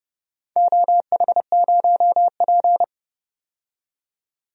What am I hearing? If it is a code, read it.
Morse code, heard as a keyed tone, about 30 wpm, 700 Hz. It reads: OH0P
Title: DXCC-100 Call Signs - 4 Characters - 30wpm